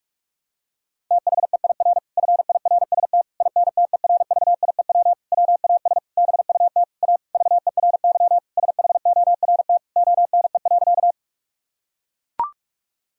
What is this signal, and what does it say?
Morse code, heard as a keyed tone, about 42 wpm, 700 Hz. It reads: THEIR FIRST INTERVIEW WAS BUT A VERY SHORT ONE.
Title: Sentences from Top 1600 Words 45wpm